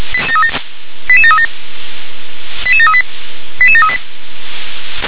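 Five-tone selective call (ZVEI), heard as quick runs of five tones, tones of about 70 ms. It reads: 88737 88738 88738
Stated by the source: ZVEI five-tone selective call received on SDR.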